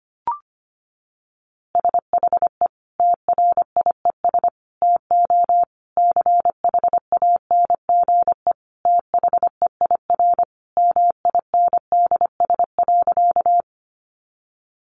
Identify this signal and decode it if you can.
Morse code, heard as a keyed tone, about 25 wpm, 700 Hz. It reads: SHE TRIES TO CHANGE THEIR MINDS.
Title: Easy Phrases 25wpm